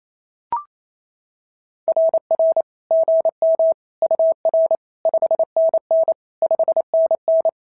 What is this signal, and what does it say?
Morse code, heard as a keyed tone, about 27 wpm, 650 Hz. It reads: RR GM UR 5NN 5NN
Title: SOTA QSO Practice 28wpm